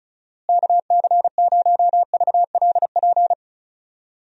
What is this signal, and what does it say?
Morse code, heard as a keyed tone, about 36 wpm, 700 Hz. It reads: KC0VLP
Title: USA Call Signs - 6 Characters - Repeated 2x - 35wpm